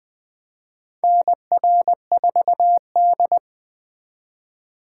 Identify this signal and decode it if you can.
Morse code, heard as a keyed tone, about 20 wpm, 700 Hz. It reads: NR4D
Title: USA Call Signs - 4 Characters - 3x Character Spacing - 20wpm